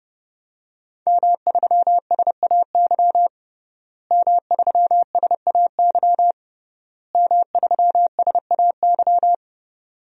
Morse code, heard as a keyed tone, about 30 wpm, 700 Hz. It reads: M3SAY M3SAY M3SAY
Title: DXCC-100 Call Signs - 5 Characters - Repeated 3x - 30wpm